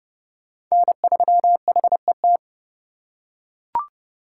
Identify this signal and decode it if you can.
Morse code, heard as a keyed tone, about 31 wpm, 700 Hz. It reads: N3HET
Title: USA Call Signs - 5 Characters - 3x Character Spacing - 30wpm